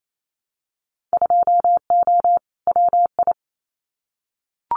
Morse code, heard as a keyed tone, about 28 wpm, 700 Hz. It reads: 2O WI
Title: Winter Field Day Exchanges 28wpm